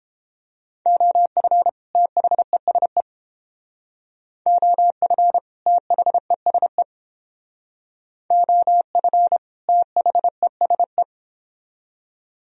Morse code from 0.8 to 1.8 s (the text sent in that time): OF